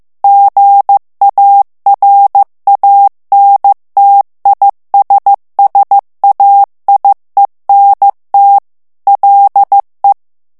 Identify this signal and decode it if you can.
Morse code, heard as a keyed tone, about 15 wpm, 800 Hz. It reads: GARANTISSAIENT LE
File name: morse mp3 / FRANCAIS TEXTE CLAIR 2 VITESSE 14 MOTS MINUTES